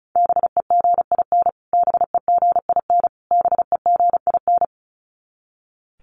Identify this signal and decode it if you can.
Morse code, heard as a keyed tone, about 34 wpm, 700 Hz. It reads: BEGIN BEGIN BEGIN
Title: Top 500 Words Repeated 35wpm